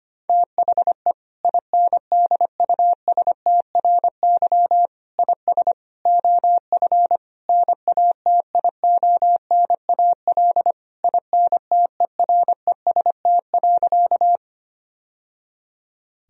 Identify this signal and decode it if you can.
Morse code, heard as a keyed tone, about 25 wpm, 700 Hz. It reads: THE INDUSTRY IS OF NATIONAL INTEREST.
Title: Sentences from Top 1000 Words - Review Part 4 - 25wpm